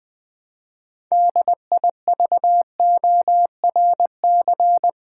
Morse code, heard as a keyed tone, about 20 wpm, 700 Hz. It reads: DIVORC
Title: Build A Word - 2 Letters At A Time - 8-Letter Words - 20wpm